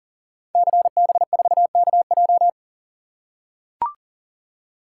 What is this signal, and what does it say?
Morse code, heard as a keyed tone, about 39 wpm, 700 Hz. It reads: CB4KJ
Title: Code Groups - 5 Letter:Numbers - 40wpm